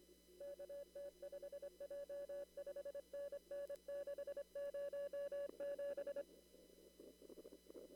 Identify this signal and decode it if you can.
Morse code, heard as a keyed tone, about 25 wpm, 570 Hz. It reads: KT5J 5NN 607
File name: CW / kt5j wpx cw-op-k5pi